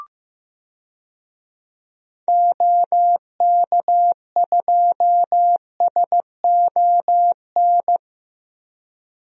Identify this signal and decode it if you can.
Morse code, heard as a keyed tone, about 15 wpm, 700 Hz. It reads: OK2SON
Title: DXCC-100 Call Signs - 6 Characters - 15wpm